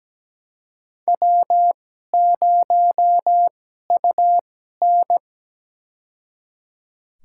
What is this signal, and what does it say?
Morse code, heard as a keyed tone, about 17 wpm, spaced as for 12 wpm, 700 Hz. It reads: W0UN
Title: USA Call Signs - 4 Characters - 2x Character Spacing - 17wpm